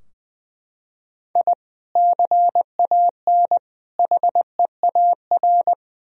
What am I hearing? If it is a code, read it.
Morse code, heard as a keyed tone, about 20 wpm, 700 Hz. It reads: I CAN HEAR
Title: Sentences from Top 1000 Words - Review Part 4 - 20wpm